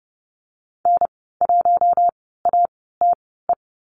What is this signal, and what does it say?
Morse code, heard as a keyed tone, about 30 wpm, spaced as for 17 wpm, 700 Hz. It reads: N1ATE
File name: Lesson 14g - Number 2 - Call Signs 30wpm